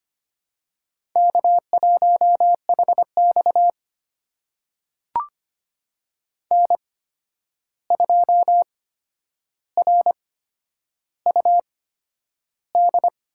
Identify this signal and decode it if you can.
Morse code, heard as a keyed tone, about 25 wpm, spaced as for 10 wpm, 700 Hz. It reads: K1HX N2RUD
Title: USA Call Signs - 8x Character Spacing - 25wpm